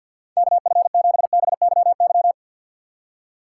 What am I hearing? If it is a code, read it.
Morse code, heard as a keyed tone, about 64 wpm, 700 Hz. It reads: KF7BYY